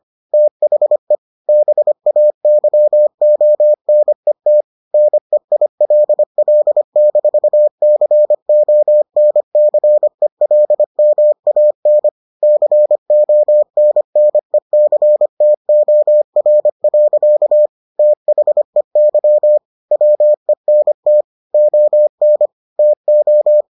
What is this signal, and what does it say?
Morse code, heard as a keyed tone, about 25 wpm, 600 Hz. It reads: THE BAYONET NEILL-CONCELMAN CONNECTOR. THEY WENT ON TO